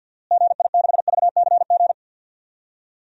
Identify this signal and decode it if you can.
Morse code, heard as a keyed tone, about 52 wpm, 700 Hz. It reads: GI6VCG
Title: DXCC-100 Call Signs - 6 Characters - Repeated 2x - 50wpm